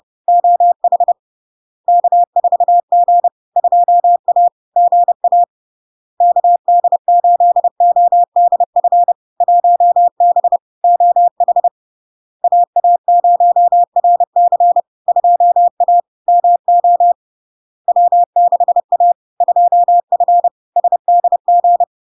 Morse code, heard as a keyed tone, about 30 wpm, 700 Hz. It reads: OH K4G 2A GA KD8ODF 1B OH AA0RC 2A MO W6A 2F SDG
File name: LARK-FD-2022-Log-in-CW-30wpm-700-Hz